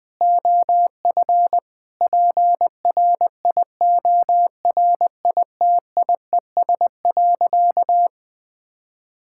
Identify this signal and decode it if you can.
Morse code, heard as a keyed tone, about 20 wpm, 700 Hz. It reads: OF PRIORITIES.
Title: Sentences with 7-Word Phrases - 20wpm